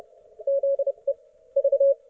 Morse code, heard as a keyed tone, about 30 wpm, spaced as for 26 wpm, 550 Hz. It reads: ZE V
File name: Archiv / CUC181 / ok1fhi cul181